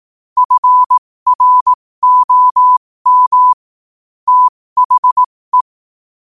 Morse code, heard as a keyed tone, about 18 wpm, spaced as for 15 wpm, 1000 Hz. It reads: FROM THE
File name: qotdfiles / 15wpm